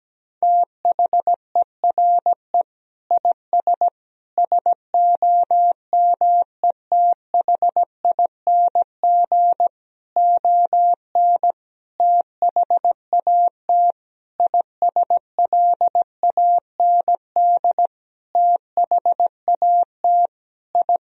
Morse code, heard as a keyed tone, about 17 wpm, 700 Hz. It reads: THERE IS SOMETHING ON THAT ISLAND THAT I
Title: Sentences from Top 1000 Words - Review Part 3 - 17wpm